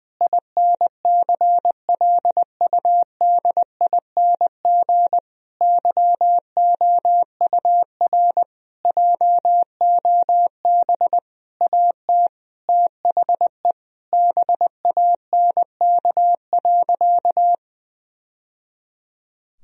Morse code, heard as a keyed tone, about 21 wpm, 700 Hz. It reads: INCLUDING YOUR JOB AT THE BANK.
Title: Sentences from Top 1000 Words - Review Part 4 - 20wpm